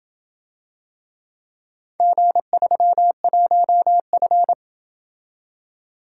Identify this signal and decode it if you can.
Morse code, heard as a keyed tone, about 27 wpm, 700 Hz. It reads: G31F